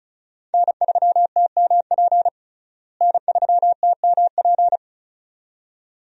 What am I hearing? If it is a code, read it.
Morse code, heard as a keyed tone, about 36 wpm, 700 Hz. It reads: N3TMP N3TMP
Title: USA Call Signs - 5 Characters - Repeated 2x - 35wpm